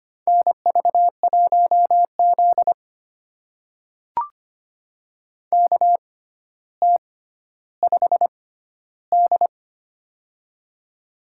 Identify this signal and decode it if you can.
Morse code, heard as a keyed tone, about 25 wpm, spaced as for 13 wpm, 700 Hz. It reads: NV1Z KT5D
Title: USA Call Signs - 4 Characters - 6x Character Spacing - 25wpm